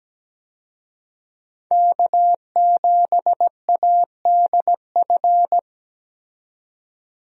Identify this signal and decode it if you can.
Morse code, heard as a keyed tone, about 17 wpm, 700 Hz. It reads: K7ADF
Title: USA Call Signs - 5 Characters - 17wpm